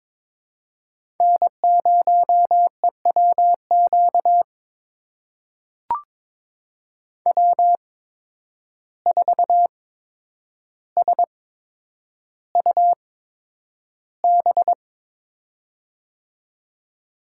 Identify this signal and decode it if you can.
Morse code, heard as a keyed tone, about 22 wpm, spaced as for 9 wpm, 700 Hz. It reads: N0EWQ W4SUB
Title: USA Call Signs - 5 Characters - 8x Character Spacing - 22wpm